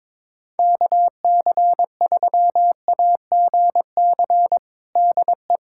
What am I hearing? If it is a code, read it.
Morse code, heard as a keyed tone, about 22 wpm, 700 Hz. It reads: KC3AGC DE
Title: USA Callsign DE Callsign 22wpm